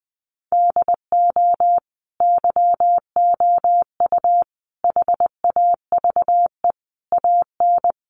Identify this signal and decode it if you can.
Morse code, heard as a keyed tone, about 20 wpm, 700 Hz. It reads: DO YOU HAVE AN
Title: Sentences from Top 1500 Words 20wpm